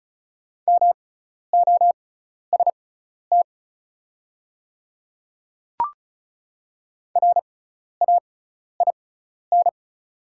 Morse code, heard as a keyed tone, about 36 wpm, spaced as for 12 wpm, 700 Hz. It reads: MOST RAIN
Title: 6x Character Spacing - Top 500 Words - 35wpm